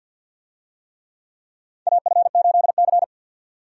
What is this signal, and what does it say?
Morse code, heard as a keyed tone, about 53 wpm, 700 Hz. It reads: AF8C
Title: USA Call Signs - 4 Characters - 50wpm